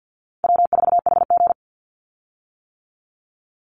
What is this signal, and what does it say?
Morse code, heard as a keyed tone, about 50 wpm, 700 Hz. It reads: W4HG
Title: Easy Call Signs - 50wpm